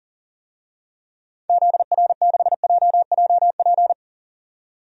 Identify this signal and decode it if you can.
Morse code, heard as a keyed tone, about 40 wpm, 700 Hz. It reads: ZR6JJP
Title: DXCC-100 Call Signs - 6 Characters - 40wpm